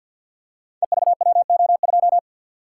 Morse code, heard as a keyed tone, about 48 wpm, 700 Hz. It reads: EVWO2
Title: Code Groups - 5 Letter:Numbers - 50wpm